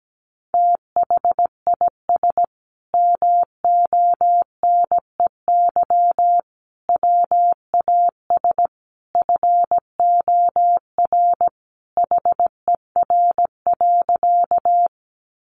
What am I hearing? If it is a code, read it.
Morse code, heard as a keyed tone, about 17 wpm, 700 Hz. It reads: THIS MONEY WAS FOR HER.
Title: Sentences from Top 400 Words 17wpm